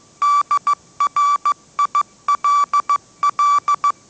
Morse code, heard as a keyed tone, about 15 wpm, 1160 Hz. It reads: DRILL
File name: mrx / mrx066 16wpm